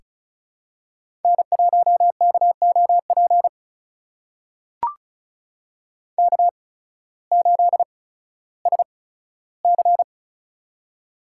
Morse code, heard as a keyed tone, about 36 wpm, spaced as for 16 wpm, 700 Hz. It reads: N1KOP K8SC
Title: USA Call Signs - 8x Character Spacing - 35wpm